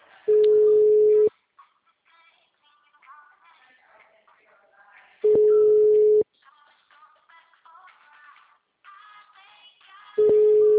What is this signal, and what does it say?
Ringback tone (Europe, 425 Hz).